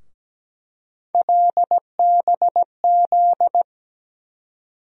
Morse code, heard as a keyed tone, about 17 wpm, 700 Hz. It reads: LBZ
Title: USA Call Sign Postfixes - 3 Letters 17wpm